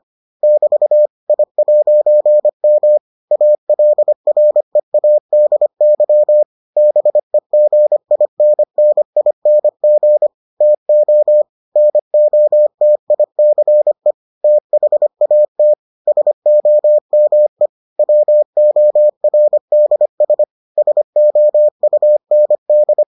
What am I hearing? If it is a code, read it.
Morse code, heard as a keyed tone, about 25 wpm, 600 Hz. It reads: <BT> I'M ALREADY BEGINNING TO NOTICE THAT SOME WORDS SOUND